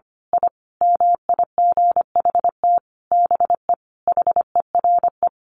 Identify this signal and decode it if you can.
Morse code, heard as a keyed tone, about 25 wpm, 700 Hz. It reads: I MIGHT BE HERE
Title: A Sound Of Thunder 25wpm